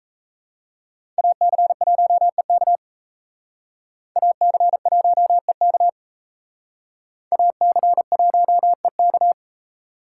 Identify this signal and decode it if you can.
Morse code, heard as a keyed tone, about 36 wpm, 700 Hz. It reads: AC1EK AC1EK AC1EK